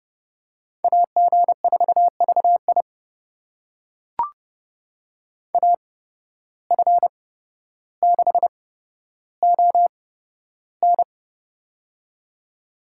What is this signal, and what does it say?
Morse code, heard as a keyed tone, about 31 wpm, spaced as for 13 wpm, 700 Hz. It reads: AG4VI AF6ON
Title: USA Call Signs - 5 Characters - 8x Character Spacing - 30wpm